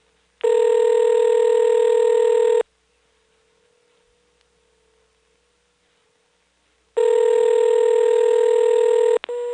Ringback tone (North America), 440 + 480 Hz.